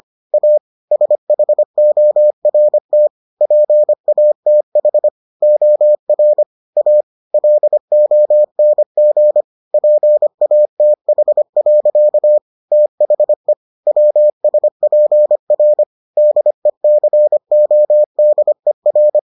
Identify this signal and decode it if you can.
Morse code, heard as a keyed tone, about 25 wpm, 600 Hz. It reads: A SHORT PATH OR A LONG PATH. THE WSPR DECODER